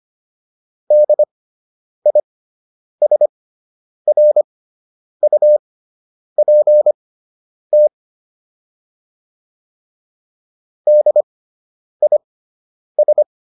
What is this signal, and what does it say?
Morse code, heard as a keyed tone, about 25 wpm, spaced as for 9 wpm, 600 Hz. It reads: DISRUPT DIS